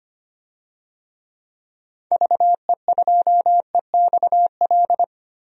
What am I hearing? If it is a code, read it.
Morse code, heard as a keyed tone, about 25 wpm, 700 Hz. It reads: VE2EXL
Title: DXCC-100 Call Signs - 6 Characters - Repeated 3x - 25wpm